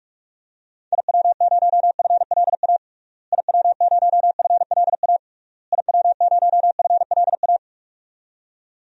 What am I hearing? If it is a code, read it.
Morse code, heard as a keyed tone, about 41 wpm, 700 Hz. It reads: IW0FLA IW0FLA IW0FLA
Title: DXCC-100 Call Signs - 6 Characters - Repeated 3x - 45wpm